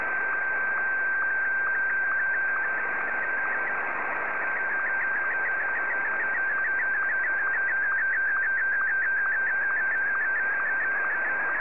SSTV (slow-scan television), Martin 1.
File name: sstv / M1-VK3DMX